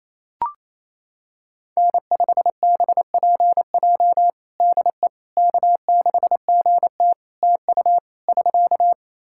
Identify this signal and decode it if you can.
Morse code, heard as a keyed tone, about 28 wpm, 700 Hz. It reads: N5BPJ DE K6GT TU <SK>
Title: K1USN SST QSO Practice 28wpm